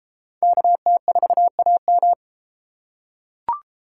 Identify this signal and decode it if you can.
Morse code, heard as a keyed tone, about 32 wpm, 700 Hz. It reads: KT4AM